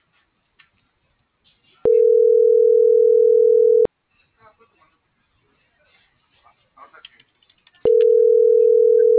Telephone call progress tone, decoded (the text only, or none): ringback tone